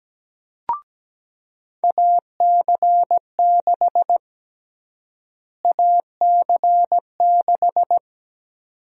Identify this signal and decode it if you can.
Morse code, heard as a keyed tone, about 17 wpm, 700 Hz. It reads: AC6 AC6
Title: USA Call Sign Prefixes - Repeated 2x - 17wpm